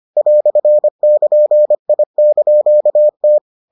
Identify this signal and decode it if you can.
Morse code, heard as a keyed tone, about 25 wpm, 600 Hz. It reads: "<KN>I)T